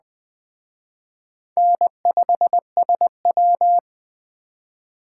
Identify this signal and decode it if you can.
Morse code, heard as a keyed tone, about 21 wpm, 700 Hz. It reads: N5SW